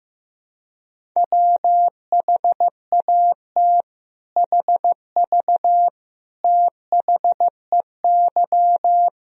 Morse code, heard as a keyed tone, about 15 wpm, 700 Hz. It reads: WHAT HV THEY
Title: Sentences from Top 500 Words and Abbreviations 15wpm